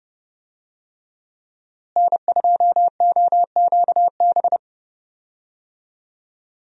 Morse code, heard as a keyed tone, about 30 wpm, 700 Hz. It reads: N2OQB
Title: USA Call Signs - 5 Characters - Repeated 3x - 30wpm